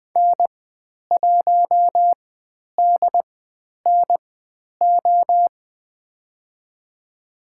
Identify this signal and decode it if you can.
Morse code, heard as a keyed tone, about 20 wpm, spaced as for 10 wpm, 700 Hz. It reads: N1DNO
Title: 095 Lesson 16g - Letter c - Call Signs 20wpm